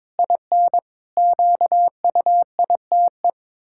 Morse code, heard as a keyed tone, about 22 wpm, 700 Hz. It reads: IN QUITE
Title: Pangram Challenge v2 - 22wpm